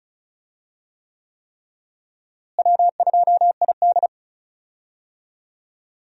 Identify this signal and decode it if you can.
Morse code, heard as a keyed tone, about 34 wpm, 700 Hz. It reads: W2ID
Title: USA Call Signs - 4 Characters - Repeated 3x - 35wpm